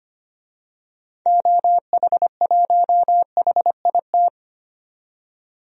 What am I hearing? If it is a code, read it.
Morse code, heard as a keyed tone, about 25 wpm, 700 Hz. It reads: OH1HIT
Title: DXCC-100 Call Signs - 6 Characters - 25wpm